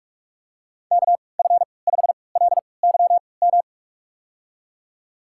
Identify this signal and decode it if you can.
Morse code, heard as a keyed tone, about 46 wpm, spaced as for 25 wpm, 700 Hz. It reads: KF5LYM
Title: USA Call Signs - 6 Characters - 3x Character Spacing - 45wpm